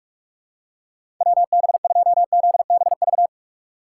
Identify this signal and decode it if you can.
Morse code, heard as a keyed tone, about 45 wpm, 700 Hz. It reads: WB2ZBV
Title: USA Call Signs - 6 Characters - 4x Character Spacing - 45wpm